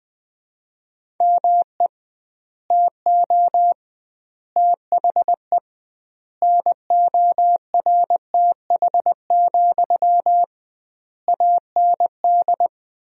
Morse code, heard as a keyed tone, about 21 wpm, 700 Hz. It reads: ME TO THE NORTH, AND
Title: 2x Word-Spacing - Sentences from Top 400 Words - 20wpm